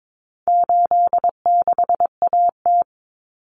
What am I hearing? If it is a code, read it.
Morse code, heard as a keyed tone, about 22 wpm, 700 Hz. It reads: 86AT
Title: Common Cut Numbers - 4-Digit 22wpm